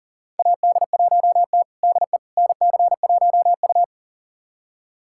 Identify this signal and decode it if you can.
Morse code, heard as a keyed tone, about 40 wpm, 700 Hz. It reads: AD1T DE NC1U
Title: USA Callsign DE Callsign - 4 Characters - 40wpm